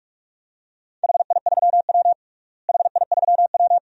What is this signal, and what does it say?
Morse code, heard as a keyed tone, about 41 wpm, 700 Hz. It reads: HI3W HI3W
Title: DXCC-100 Call Signs - 4 Characters - Repeated 2x - 45wpm